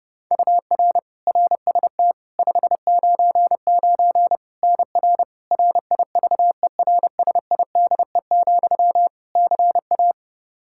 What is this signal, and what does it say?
Morse code, heard as a keyed tone, about 30 wpm, 700 Hz. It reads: UR RST 599 NR RIVERSIDE, CA
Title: Practice QSO - Exchange 1 - Beginning - 30wpm